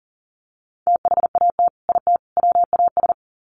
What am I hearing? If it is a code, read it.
Morse code, heard as a keyed tone, about 40 wpm, 700 Hz. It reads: THAT IT WAS
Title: Sets of 3 Words 40wpm